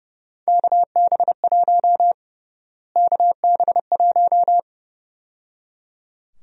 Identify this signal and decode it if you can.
Morse code, heard as a keyed tone, about 30 wpm, 700 Hz. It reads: KB1 KB1
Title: USA Call Sign Prefixes - Repeated 2x - 30wpm